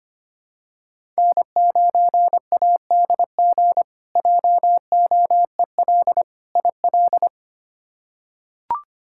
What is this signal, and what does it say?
Morse code, heard as a keyed tone, about 25 wpm, 700 Hz. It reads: N9ADG JOEL IL
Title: K1USN SST QSO Practice - 25wpm